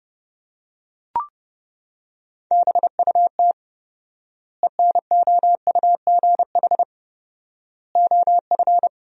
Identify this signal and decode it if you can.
Morse code, heard as a keyed tone, about 30 wpm, 700 Hz. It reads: BUT ENOUGH OF
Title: 4x Word-Spacing - Sentences from Top 400 Words - 30wpm